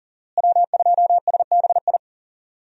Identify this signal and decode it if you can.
Morse code, heard as a keyed tone, about 40 wpm, 700 Hz. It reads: W2SBI
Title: USA Call Signs - 5 Characters - Repeated 3x - 40wpm